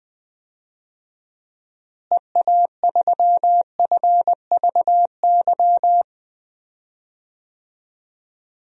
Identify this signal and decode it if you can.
Morse code, heard as a keyed tone, about 20 wpm, 700 Hz. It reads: EA3FVY
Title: DXCC-100 Call Signs - 6 Characters - Repeated 3x - 20wpm